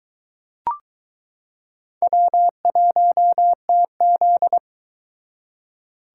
Morse code, heard as a keyed tone, about 22 wpm, 700 Hz. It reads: W1TZ